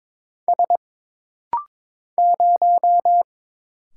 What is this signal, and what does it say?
Morse code, heard as a keyed tone, about 22 wpm, 700 Hz. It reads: S 0
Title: Single Letter-Number - Rapid-Fire 22wpm